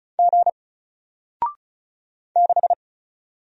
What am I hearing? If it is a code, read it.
Morse code, heard as a keyed tone, about 34 wpm, 700 Hz. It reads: G 6
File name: Single-Letter-Number-35wpm